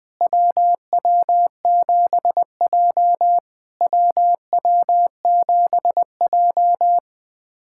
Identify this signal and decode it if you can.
Morse code, heard as a keyed tone, about 20 wpm, 700 Hz. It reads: WW7J WW7J
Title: SOTA QSO Practice 20wpm